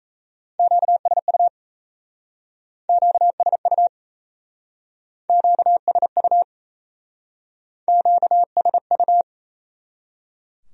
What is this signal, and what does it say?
Morse code, heard as a keyed tone, about 35 wpm, 700 Hz. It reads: QSU QSU QSU QSU